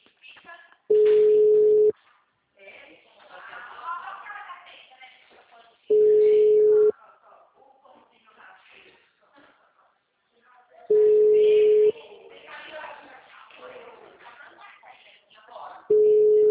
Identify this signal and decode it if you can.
Ringback tone (Europe, 425 Hz), with speech.
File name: out- -2016- - - .50455